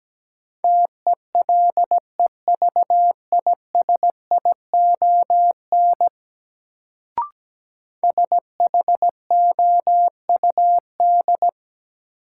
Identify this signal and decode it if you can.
Morse code, heard as a keyed tone, about 17 wpm, 700 Hz. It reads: TELEVISION SHOUD
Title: Sentences from Top 1000 Words - Review Part 4 - New Word Drill - 17wpm